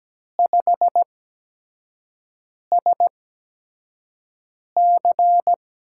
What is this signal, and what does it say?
Morse code, heard as a keyed tone, about 17 wpm, spaced as for 5 wpm, 700 Hz. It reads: 5SC
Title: USA Call Signs - 8x Character Spacing - 17wpm